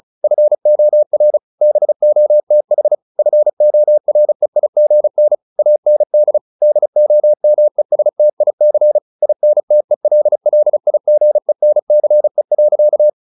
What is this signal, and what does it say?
Morse code, heard as a keyed tone, about 34 wpm, 600 Hz. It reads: FOR BOTH FOREIGN AND DOMESTIC INTELLIGENCE.